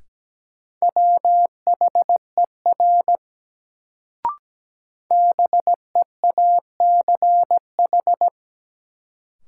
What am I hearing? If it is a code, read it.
Morse code, heard as a keyed tone, about 17 wpm, 700 Hz. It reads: WHER BEACH
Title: Sentences from Top 1500 Words - New Word Drill Alt - 17wpm